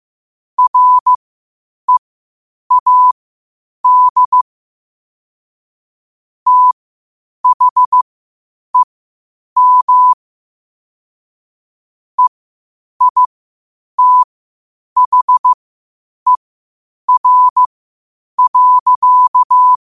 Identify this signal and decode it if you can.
Morse code, heard as a keyed tone, about 15 wpm, spaced as for 8 wpm, 1000 Hz. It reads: READ THEM EITHER.
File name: qotdfiles / 8wpm